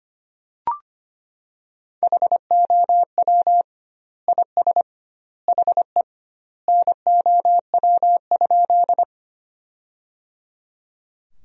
Morse code, heard as a keyed tone, about 25 wpm, 700 Hz. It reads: HOW IS HE NOW?
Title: 2x Word-Spacing - Sentences from Top 100 Words - 25wpm